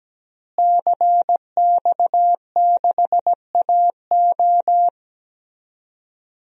Morse code, heard as a keyed tone, about 17 wpm, 700 Hz. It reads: CX6AO
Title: DXCC-100 Call Signs - 5 Characters - Repeated 3x - 17wpm